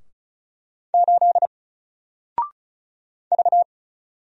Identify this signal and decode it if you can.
Morse code, heard as a keyed tone, about 36 wpm, 700 Hz. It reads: 8 V
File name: Single-Letter-Number-35wpm